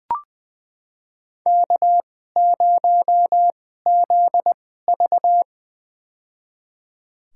Morse code, heard as a keyed tone, about 20 wpm, spaced as for 15 wpm, 700 Hz. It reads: K0ZV
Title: USA Call Signs - 4 Characters - 2x Character Spacing - 20wpm